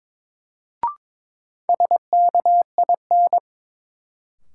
Morse code, heard as a keyed tone, about 22 wpm, 700 Hz. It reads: SKIN
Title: Sentences from Top 1300 Words - New Word Drill Alt - 22wpm